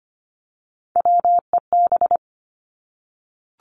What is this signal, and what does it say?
Morse code, heard as a keyed tone, about 25 wpm, 700 Hz. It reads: WEB
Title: Top 501-750 Words 25wpm